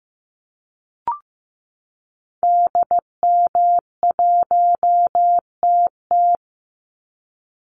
Morse code, heard as a keyed tone, about 15 wpm, 700 Hz. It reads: DM1TT
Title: Call Signs Repeated 3x - 15wpm